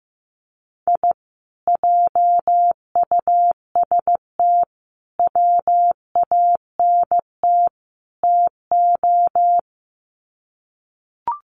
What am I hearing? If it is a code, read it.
Morse code, heard as a keyed tone, about 15 wpm, 700 Hz. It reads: I JUST WANT TO
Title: Sets of 4 Words 15wpm